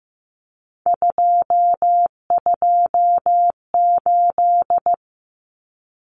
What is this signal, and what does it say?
Morse code, heard as a keyed tone, about 15 wpm, 700 Hz. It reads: 228
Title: Numbers - 3-Digit 15wpm